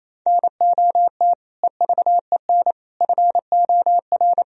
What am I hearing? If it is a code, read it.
Morse code, heard as a keyed tone, about 28 wpm, 700 Hz. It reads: NOT EVEN FOR
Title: Sentences from Top 1000 Words - Review Part 3 - 28wpm